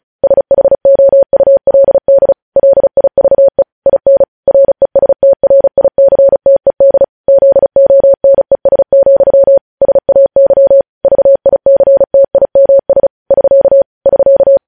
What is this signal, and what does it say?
Morse code, heard as a keyed tone, about 36 wpm, 550 Hz. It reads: SHOULD LIVE IN RESTRICTED ZONES, SAY VICTIMS <SK> <SK>
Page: news headlines in morse code at 35 wpm